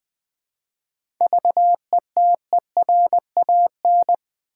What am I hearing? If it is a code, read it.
Morse code, heard as a keyed tone, about 20 wpm, 700 Hz. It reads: VETERAN
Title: 7 Letter Words - First 3 Letters Given - 20wpm